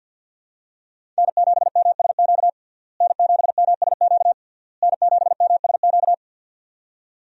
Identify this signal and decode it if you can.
Morse code, heard as a keyed tone, about 50 wpm, 700 Hz. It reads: N7MSQ N7MSQ N7MSQ
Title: USA Call Signs - 5 Characters - Repeated 3x - 50wpm